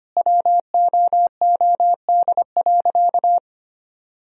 Morse code, heard as a keyed tone, about 25 wpm, 700 Hz. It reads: WOOD.
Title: Sentences from Top 1000 Words - Review Part 3 - 25wpm